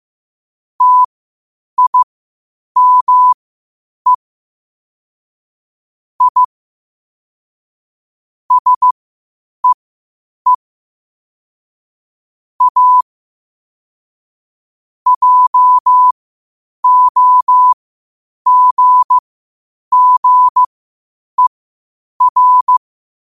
Morse code, heard as a keyed tone, about 15 wpm, spaced as for 8 wpm, 1000 Hz. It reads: TIME I SEE A JOGGER
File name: qotdfiles / 8wpm